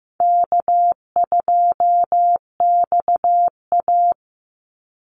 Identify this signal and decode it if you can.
Morse code, heard as a keyed tone, about 15 wpm, 700 Hz. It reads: K2XA
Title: Easy Call Signs - 15wpm